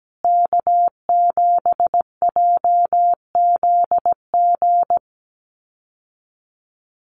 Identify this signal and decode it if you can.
Morse code, heard as a keyed tone, about 17 wpm, 700 Hz. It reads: K7JZG
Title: USA Call Signs Repeated 3x 17wpm